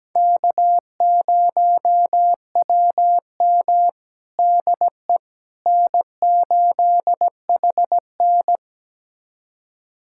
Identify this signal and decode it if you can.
Morse code, heard as a keyed tone, about 17 wpm, 700 Hz. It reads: K0WM DE N8HN
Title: USA Callsign DE Callsign - 4 Characters - 17wpm